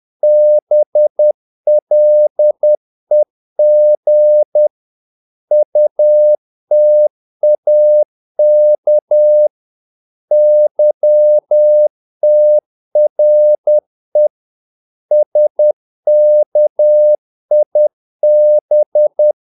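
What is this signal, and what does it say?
Morse code, heard as a keyed tone, about 10 wpm, 600 Hz. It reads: BLEG UTAK YTRE SKIB
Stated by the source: Morse practice of short Danish words at 10 wpm, from CW med Gnister.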